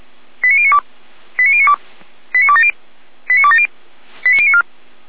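Five-tone selective call (ZVEI), heard as quick runs of five tones, tones of about 70 ms. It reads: 89092 89092 89270 89270 89094
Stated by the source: ZVEI five-tone selective call 89020 received on SDR.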